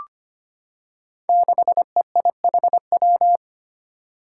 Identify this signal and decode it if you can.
Morse code, heard as a keyed tone, about 25 wpm, 700 Hz. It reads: 6EIHW
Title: Five Challenging Letter-Number Combinations 25wpm